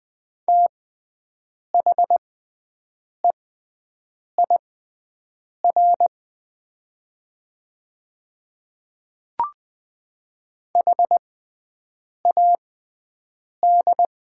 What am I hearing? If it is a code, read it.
Morse code, heard as a keyed tone, about 20 wpm, spaced as for 7 wpm, 700 Hz. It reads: THEIR HAD
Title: 6x Character Spacing - Top 200 Words - 20wpm